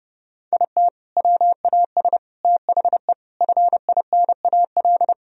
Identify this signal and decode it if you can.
Morse code, heard as a keyed tone, about 30 wpm, 700 Hz. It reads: IT WAS THE FINAL
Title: Sentences from Top 1000 Words - Review Part 2 - 30wpm